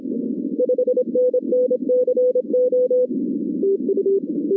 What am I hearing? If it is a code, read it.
Morse code, heard as a keyed tone, about 26 wpm, 500 Hz, with noise. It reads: 5NN CO
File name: 2016 arrl dx cw / 2016-ARRL-DX-CW-K5TR- - 0-22